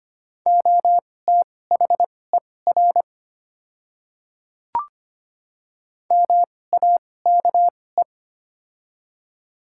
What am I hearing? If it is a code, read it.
Morse code, heard as a keyed tone, about 25 wpm, spaced as for 18 wpm, 700 Hz. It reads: OTHER MAKE
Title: 2x Character Spacing - Top 100 Words - 25wpm